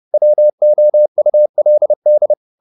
Morse code, heard as a keyed tone, about 30 wpm, 600 Hz. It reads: WOULD